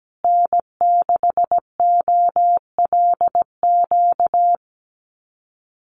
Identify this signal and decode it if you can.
Morse code, heard as a keyed tone, about 17 wpm, 700 Hz. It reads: N6OLQ
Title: USA Call Signs Repeated 3x 17wpm